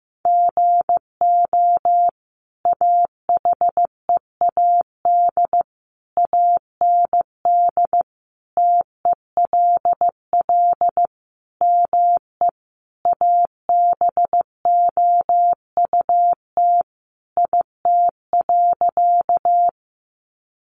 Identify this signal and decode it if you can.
Morse code, heard as a keyed tone, about 15 wpm, 700 Hz. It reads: GO AHEAD AND TELL ME ABOUT IT.
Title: Sentences from Top 1100 Words 15wpm